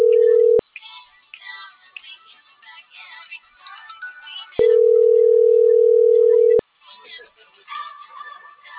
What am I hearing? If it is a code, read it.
Ringback tone (North America), 440 + 480 Hz.